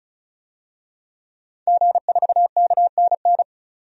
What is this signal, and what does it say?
Morse code, heard as a keyed tone, about 35 wpm, 700 Hz. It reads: G4KNN
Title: DXCC-100 Call Signs - 5 Characters - 35wpm